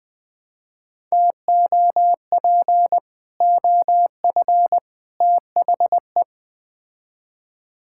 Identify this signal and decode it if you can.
Morse code, heard as a keyed tone, about 20 wpm, 700 Hz. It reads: TOP OF THE
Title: Build A Phrase - 3-Word Phrases - 20wpm